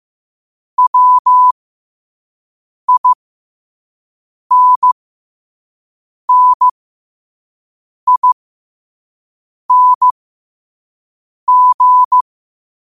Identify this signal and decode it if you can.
Morse code, heard as a keyed tone, about 15 wpm, spaced as for 5 wpm, 1000 Hz. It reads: WINNING
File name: qotdfiles / 5wpm